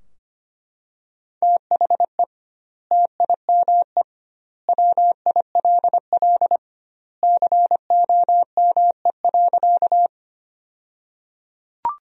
Morse code, heard as a keyed tone, about 25 wpm, 700 Hz. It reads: THE TIME WILL COME.
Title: 2x Word-Spacing - Sentences from Top 100 Words - 25wpm